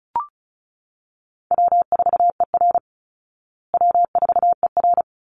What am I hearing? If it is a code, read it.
Morse code, heard as a keyed tone, about 34 wpm, 700 Hz. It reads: W4ER W4ER
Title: Easy Call Signs Repeated 2x - 35wpm